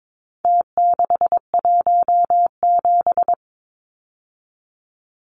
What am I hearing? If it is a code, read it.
Morse code, heard as a keyed tone, about 22 wpm, 700 Hz. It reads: T617
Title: Numbers - Contest Style - 4-Digit 22wpm